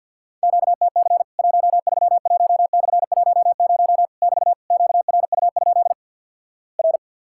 Morse code, heard as a keyed tone, about 48 wpm, 700 Hz. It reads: QTC 131/10 <BT> QRU? R
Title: Worked All Europe DX Contest - QTC Exchange with Repeats - 50wpm